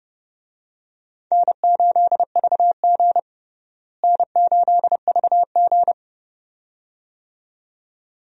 Morse code, heard as a keyed tone, about 30 wpm, 700 Hz. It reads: N8VG N8VG
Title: USA Call Signs - 4 Characters - Repeated 2x - 30wpm